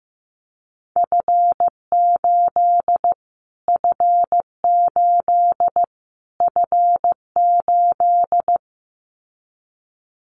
Morse code, heard as a keyed tone, about 15 wpm, 700 Hz. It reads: F8 F8 F8
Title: Callsign Prefixes for Top 25 DXCC - Repeated 3x - 15wpm